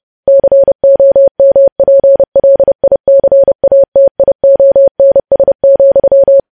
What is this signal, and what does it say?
Morse code, heard as a keyed tone, about 30 wpm, 550 Hz. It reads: COMPLICATIONS,
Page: news headlines in morse code at 30 wpm